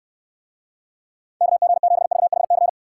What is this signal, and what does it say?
Morse code, heard as a keyed tone, about 69 wpm, 700 Hz. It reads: KG7VHC